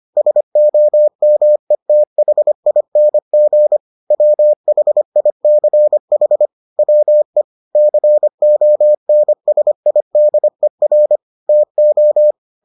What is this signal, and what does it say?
Morse code, heard as a keyed tone, about 25 wpm, 600 Hz. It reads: SOMETHING WHICH WE CONSIDER TO